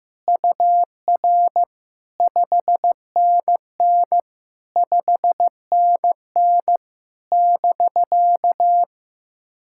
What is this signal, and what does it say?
Morse code, heard as a keyed tone, about 15 wpm, 700 Hz. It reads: UR 5NN 5NN <BK>
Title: SOTA QSO Practice 15wpm